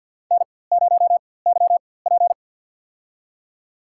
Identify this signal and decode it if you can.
Morse code, heard as a keyed tone, about 48 wpm, spaced as for 23 wpm, 700 Hz. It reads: N0YP
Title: USA Call Signs - 4x Character Spacing - 50wpm